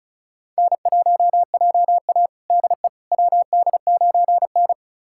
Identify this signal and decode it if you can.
Morse code, heard as a keyed tone, about 36 wpm, 700 Hz. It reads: N1JA DE WD9N
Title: USA Callsign DE Callsign - 4 Characters - 35wpm